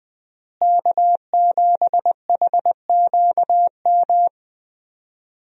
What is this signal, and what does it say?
Morse code, heard as a keyed tone, about 20 wpm, 700 Hz. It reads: K7HQM
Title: USA Call Signs - 5 Characters - Repeated 2x - 20wpm